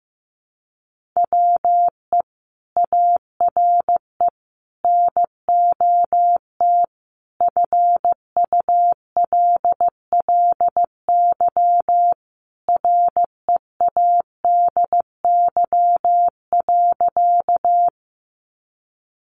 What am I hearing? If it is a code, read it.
Morse code, heard as a keyed tone, about 15 wpm, 700 Hz. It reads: WE ARE NOT FULLY READY.
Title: Sentences from Top 1300 Words 15wpm